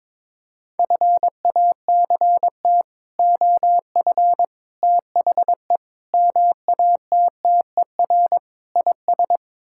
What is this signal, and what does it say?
Morse code, heard as a keyed tone, about 22 wpm, 700 Hz. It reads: FACT OF THE MATTER IS
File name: practice / 22wpm / Build A Phrase - 5-Word Phrases - 22wpm